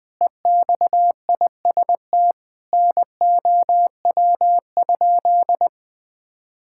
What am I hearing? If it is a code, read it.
Morse code, heard as a keyed tone, about 20 wpm, 700 Hz. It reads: EXIST NOW?
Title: Sentences from Top 1000 Words - Review Part 5 - 20wpm